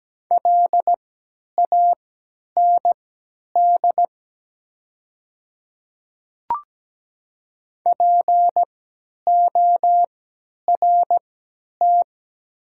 Morse code, heard as a keyed tone, about 17 wpm, spaced as for 10 wpm, 700 Hz. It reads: LAND PORT
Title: 3x Character Spacing - Top 300 Words - 17wpm